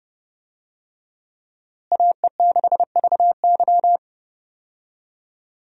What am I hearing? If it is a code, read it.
Morse code, heard as a keyed tone, about 30 wpm, 700 Hz. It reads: AE6VY
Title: USA Call Signs - 5 Characters - Repeated 3x - 30wpm